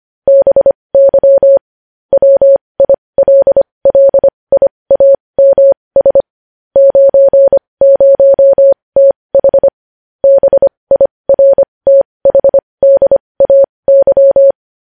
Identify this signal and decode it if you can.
Morse code, heard as a keyed tone, about 25 wpm, spaced as for 20 wpm, 550 Hz. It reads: BY WILLIAMS 90TH BIRTHDAY
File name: podcast / News headlines-1x-25x20